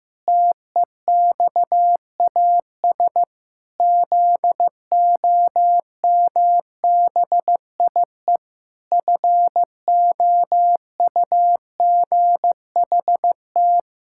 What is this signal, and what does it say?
Morse code, heard as a keyed tone, about 15 wpm, 700 Hz. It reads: TEXAS ZOMBIE FOUGHT
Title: Pangram Challenge v2 - 15wpm